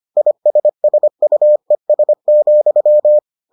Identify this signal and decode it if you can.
Morse code, heard as a keyed tone, about 25 wpm, 600 Hz. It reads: ISSUES,